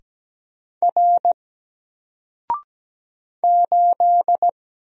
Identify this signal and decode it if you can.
Morse code, heard as a keyed tone, about 17 wpm, 700 Hz. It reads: R 8
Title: Single Character (Letter:Number:Punctuation:Prosign) Alt 17wpm